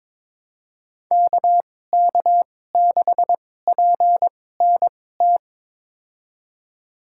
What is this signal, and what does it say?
Morse code, heard as a keyed tone, about 21 wpm, spaced as for 16 wpm, 700 Hz. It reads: KK6PNT
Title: USA Call Signs - 2x Character Spacing - 22wpm